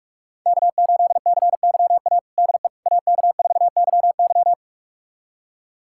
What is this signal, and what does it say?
Morse code, heard as a keyed tone, about 46 wpm, 700 Hz. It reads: K8CYA DE AK4YY
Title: USA Callsign DE Callsign - 5 Characters - 45wpm